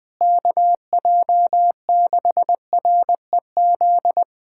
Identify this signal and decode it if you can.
Morse code, heard as a keyed tone, about 20 wpm, 700 Hz. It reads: KJ6REZ
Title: USA Callsign DE Callsign - 6 Characters - 20wpm